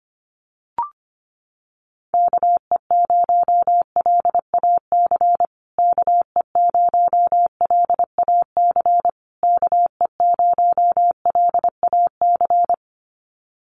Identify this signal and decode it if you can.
Morse code, heard as a keyed tone, about 25 wpm, 700 Hz. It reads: KE0LAC KE0LAC KE0LAC
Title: USA Call Signs Repeated 3x 25wpm